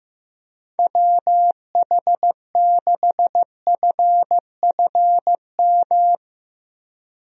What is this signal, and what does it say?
Morse code, heard as a keyed tone, about 16 wpm, 700 Hz. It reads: WH6FFM
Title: USA Call Signs - 6 Characters - 15wpm